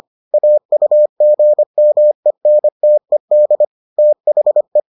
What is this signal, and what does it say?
Morse code, heard as a keyed tone, about 25 wpm, 600 Hz. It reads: AUGMENTED THE